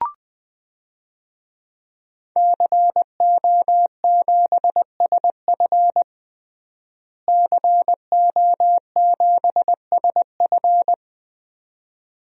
Morse code, heard as a keyed tone, about 20 wpm, 700 Hz. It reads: CO7SF CO7SF
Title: DXCC-100 Call Signs - 5 Characters - Repeated 2x - 20wpm